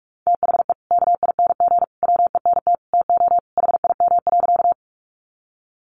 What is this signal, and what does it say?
Morse code, heard as a keyed tone, about 44 wpm, 700 Hz. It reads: THE KING WENT TO HIM.
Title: Sentences from Top 400 Words 45wpm